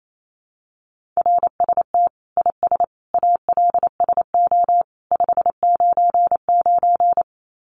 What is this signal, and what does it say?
Morse code, heard as a keyed tone, about 28 wpm, 700 Hz. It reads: RST IS ALSO 599
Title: Common QSO Phrases 28wpm